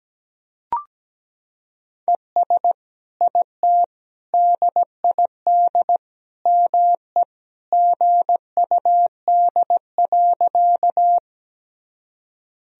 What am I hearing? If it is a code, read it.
Morse code, heard as a keyed tone, about 17 wpm, 700 Hz. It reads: ES IT DID ME GUD.
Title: Sentences from Top 200 Words and Abbreviations 17wpm